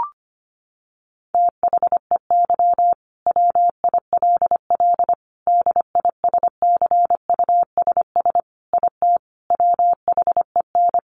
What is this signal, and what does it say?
Morse code, heard as a keyed tone, about 25 wpm, 700 Hz. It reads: THEY WILL DISCUSS IT WHEN
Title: Sentences from Top 1400 Words 25wpm